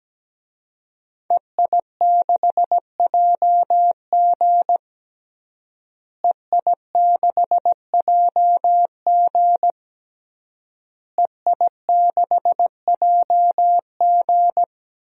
Morse code, heard as a keyed tone, about 17 wpm, 700 Hz. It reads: EI6JG EI6JG EI6JG
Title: DXCC-100 Call Signs - 5 Characters - Repeated 3x - 17wpm